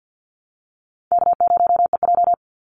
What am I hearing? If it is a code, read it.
Morse code, heard as a keyed tone, about 50 wpm, 700 Hz. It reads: K0EJ
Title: Call Signs Repeated 2x - 50wpm